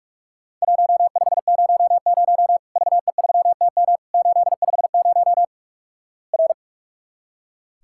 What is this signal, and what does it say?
Morse code, heard as a keyed tone, about 46 wpm, 700 Hz. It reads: 1500 VE3TM 850 R
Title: Worked All Europe DX Contest - QTC Exchange - 45wpm